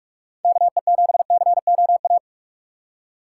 Morse code, heard as a keyed tone, about 42 wpm, 700 Hz. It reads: KE7COA
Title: USA Call Signs - 6 Characters - 3x Character Spacing - 45wpm